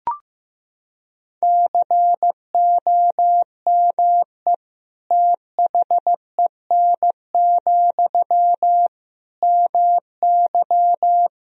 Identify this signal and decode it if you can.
Morse code, heard as a keyed tone, about 15 wpm, 700 Hz. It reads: COME THEN, MY
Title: Sentences from Top 1000 Words - Review Part 2 - 15wpm